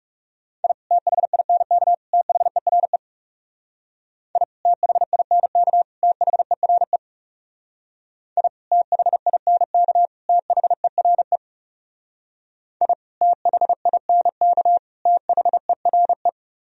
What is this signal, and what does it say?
Morse code, heard as a keyed tone, about 37 wpm, 700 Hz. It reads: I THINK THERE I THINK THERE I THINK THERE I THINK THERE
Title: Speed-Racing - Sets of 3 Words 30wpm